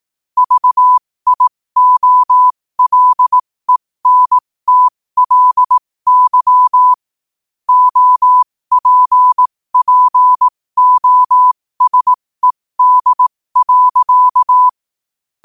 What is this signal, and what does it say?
Morse code, heard as a keyed tone, about 18 wpm, spaced as for 15 wpm, 1000 Hz. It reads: VIOLENTLY OPPOSED.
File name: qotdfiles / 15wpm